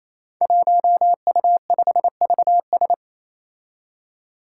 Morse code, heard as a keyed tone, about 27 wpm, 700 Hz. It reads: 1U5VS
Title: Five Challenging Letter-Number Combinations 28wpm